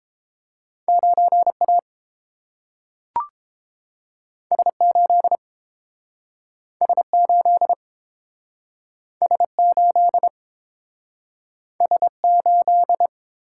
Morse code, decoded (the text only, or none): 9A S8 S8 S8 S8